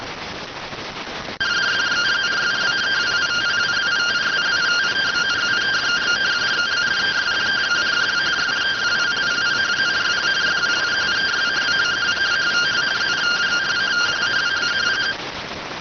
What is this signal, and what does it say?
Two-tone teleprinter signal, about 170 Hz shift.